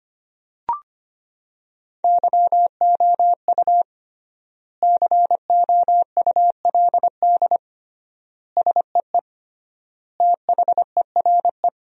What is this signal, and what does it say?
Morse code, heard as a keyed tone, about 25 wpm, 700 Hz. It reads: YOU COULD SEE THERE
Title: 3x Word-Spacing - Sentences from Top 500 Words - 25wpm